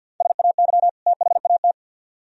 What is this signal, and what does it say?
Morse code, heard as a keyed tone, about 57 wpm, 700 Hz. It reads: SAY THAT
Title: Speed-Racing - Sets of 2 Words 50wpm